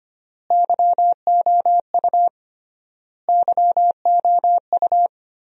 Morse code, heard as a keyed tone, about 25 wpm, 700 Hz. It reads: YOU YOU
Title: 3x Word-Spacing - Sentences from Top 400 Words - 25wpm